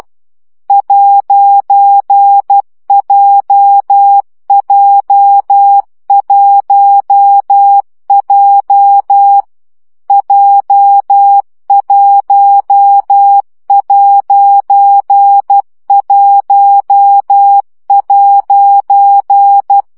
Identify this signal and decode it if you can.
Morse code, heard as a keyed tone, about 12 wpm, 800 Hz. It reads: 'JJ1J J1'1'
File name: CW gyakorlas j 1 aposztrof 10 baud